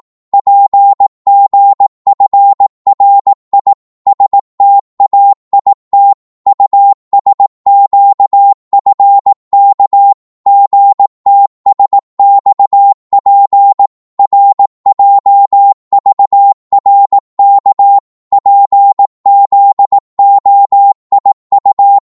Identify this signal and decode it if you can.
Morse code, heard as a keyed tone, about 18 wpm, 800 Hz. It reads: PGFRI STAIT USQFK GTSXP RJVRK PZOIU
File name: Morse / ltr-18wpm